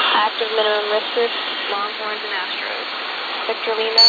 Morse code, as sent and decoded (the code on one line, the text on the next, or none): none